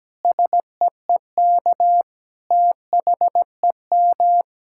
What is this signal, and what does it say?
Morse code, heard as a keyed tone, about 17 wpm, 700 Hz. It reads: SEEK THEM
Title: Sentences from Top 1600 Words 17wpm